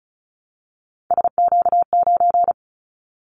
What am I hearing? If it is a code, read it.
Morse code, heard as a keyed tone, about 34 wpm, 700 Hz. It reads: SQ9
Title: Callsign Prefixes for Top 25 DXCC - 35wpm